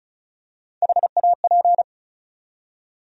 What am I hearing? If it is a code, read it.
Morse code, heard as a keyed tone, about 36 wpm, 700 Hz. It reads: HAP
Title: USA Call Sign Postfixes - 3 Letters 35wpm